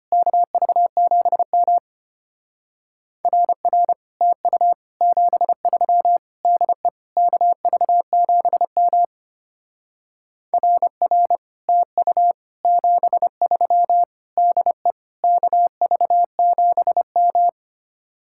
Morse code, heard as a keyed tone, about 31 wpm, 700 Hz. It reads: KV7M RR TU 73 DE KV7M RR TU 73 DE KV7M
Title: Speed-Racing Common QSO Phrases - 25wpm